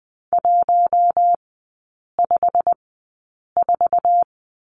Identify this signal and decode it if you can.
Morse code, heard as a keyed tone, about 20 wpm, spaced as for 8 wpm, 700 Hz. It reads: 154
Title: 087 Lesson 15e - Number 5 - Three Characters 20wpm